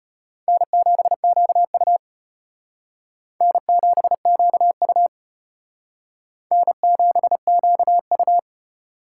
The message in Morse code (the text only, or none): N7QU N7QU N7QU